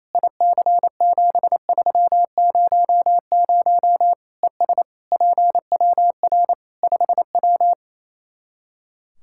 Morse code, heard as a keyed tone, about 28 wpm, 700 Hz. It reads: IC7300 ES PWR 5W
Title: Practice QSO - Exchange 2 - Info - 28wpm